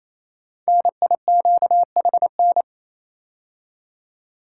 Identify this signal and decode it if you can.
Morse code, heard as a keyed tone, about 27 wpm, 700 Hz. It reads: NIQHN
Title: Code Groups - 5 Letter:Numbers - 28wpm